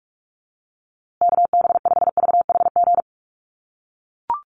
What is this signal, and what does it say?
Morse code, heard as a keyed tone, about 44 wpm, 700 Hz. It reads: KB5VHG
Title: USA Call Signs Repeated 3x 45wpm